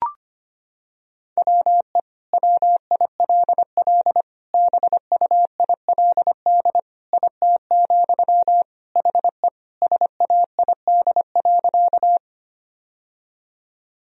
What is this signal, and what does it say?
Morse code, heard as a keyed tone, about 25 wpm, 700 Hz. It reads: WE WILL BUILD IT, HE SAID.
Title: Sentences from Top 1000 Words - Review Part 1 - 25wpm